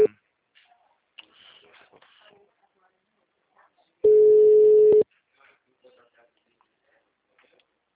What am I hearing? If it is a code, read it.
Ringback tone (Europe, 425 Hz), with speech.